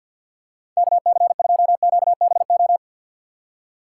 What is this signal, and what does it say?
Morse code, heard as a keyed tone, about 46 wpm, 690 Hz. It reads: KC2QBO
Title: USA Call Signs - 3x Character Spacing - 50wpm